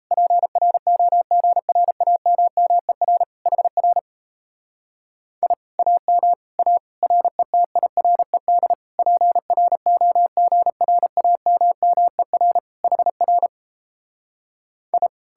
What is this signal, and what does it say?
Morse code, heard as a keyed tone, about 37 wpm, 700 Hz. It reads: PROGRAMMER HR I AM A RETIRED PROGRAMMER HR I